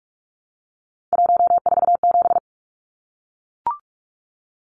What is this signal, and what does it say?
Morse code, heard as a keyed tone, about 44 wpm, 700 Hz. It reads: 147
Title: Numbers - 3-Digit 45wpm